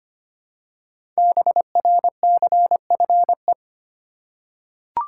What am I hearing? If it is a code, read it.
Morse code, heard as a keyed tone, about 25 wpm, 700 Hz. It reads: BRCFE
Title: Code Groups - 5 Letters - 25wpm